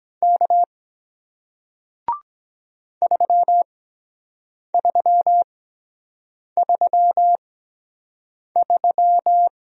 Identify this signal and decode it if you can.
Morse code, heard as a keyed tone, about 23 wpm, spaced as for 7 wpm, 700 Hz. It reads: K 3333